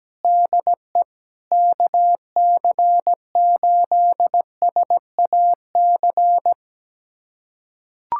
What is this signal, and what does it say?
Morse code, heard as a keyed tone, about 17 wpm, 700 Hz. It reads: DE KC8SAC
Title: USA Callsign DE Callsign 17wpm